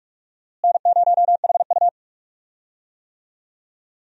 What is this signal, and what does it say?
Morse code, heard as a keyed tone, about 41 wpm, 700 Hz. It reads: N0HU
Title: USA Call Signs - 4 Characters - Repeated 3x - 45wpm